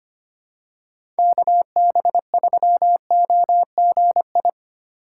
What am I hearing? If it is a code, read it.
Morse code, heard as a keyed tone, about 25 wpm, 700 Hz. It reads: KB3OGI
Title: USA Call Signs - 6 Characters - 3x Character Spacing - 25wpm